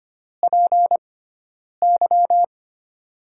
Morse code, heard as a keyed tone, about 25 wpm, 690 Hz. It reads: P Y
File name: Top 90 QSO Elements - 6x Character Spacing - 25wpm